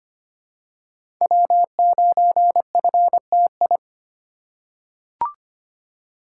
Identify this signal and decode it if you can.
Morse code, heard as a keyed tone, about 25 wpm, 700 Hz. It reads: W9FTI
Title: USA Call Signs - 5 Characters - 2x Character Spacing - 25wpm